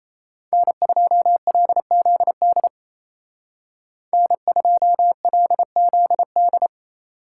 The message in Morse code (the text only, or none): N2LZD N2LZD